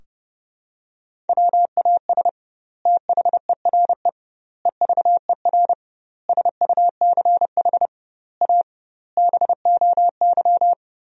Morse code, heard as a keyed tone, about 32 wpm, 700 Hz. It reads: WAS THERE EVER SUCH A BOY
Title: 2x Word-Spacing - Sentences from Top 400 Words - 30wpm